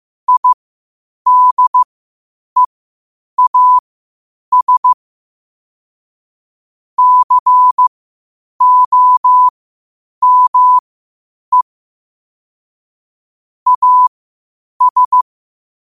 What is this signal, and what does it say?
Morse code, heard as a keyed tone, about 15 wpm, spaced as for 8 wpm, 1000 Hz. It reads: IDEAS COME AS
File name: qotdfiles / 8wpm